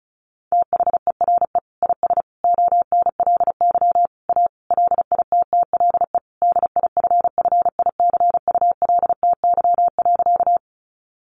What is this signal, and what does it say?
Morse code, heard as a keyed tone, about 34 wpm, 700 Hz. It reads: THERE IS ONLY A LITTLE DIFFICULTY.
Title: Sentences from Top 1500 Words 35wpm